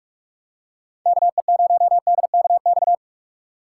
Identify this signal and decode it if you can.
Morse code, heard as a keyed tone, about 44 wpm, 700 Hz. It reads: KE0DKX
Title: USA Call Signs - 6 Characters - 6x Character Spacing - 45wpm